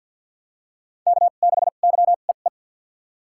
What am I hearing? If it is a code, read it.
Morse code, heard as a keyed tone, about 47 wpm, spaced as for 35 wpm, 700 Hz. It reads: K6YEE
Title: USA Call Signs - 5 Characters - 2x Character Spacing - 50wpm